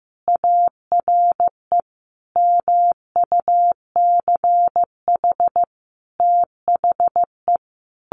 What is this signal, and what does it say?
Morse code, heard as a keyed tone, about 15 wpm, 700 Hz. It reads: ARE MUCH THE
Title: Sentences from Top 1100 Words 15wpm